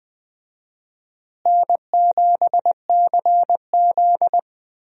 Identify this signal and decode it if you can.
Morse code, heard as a keyed tone, about 20 wpm, 700 Hz. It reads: N7CZ
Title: USA Call Signs - 4 Characters - Repeated 3x - 20wpm